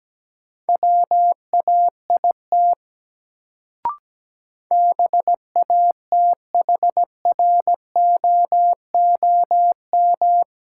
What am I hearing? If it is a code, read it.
Morse code, heard as a keyed tone, about 17 wpm, 700 Hz. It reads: WAIT BATHROOM
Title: Sets of 5 Words Encore - New Word Drill - 17wpm